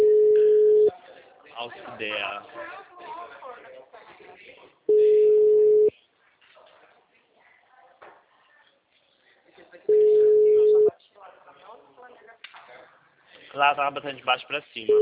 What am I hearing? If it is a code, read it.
Ringback tone (Europe, 425 Hz), with speech.